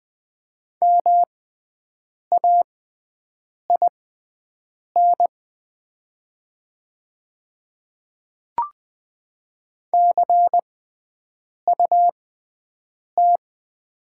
Morse code, heard as a keyed tone, about 20 wpm, spaced as for 7 wpm, 700 Hz. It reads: MAIN CUT
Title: 6x Character Spacing - Top 300 Words - 20wpm